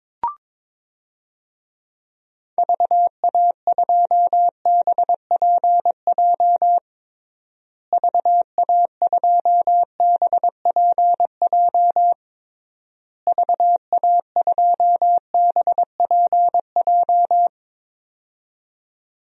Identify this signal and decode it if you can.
Morse code, heard as a keyed tone, about 22 wpm, 700 Hz. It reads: VA2BPJ VA2BPJ VA2BPJ
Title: DXCC-100 Call Signs - 6 Characters - Repeated 3x - 22wpm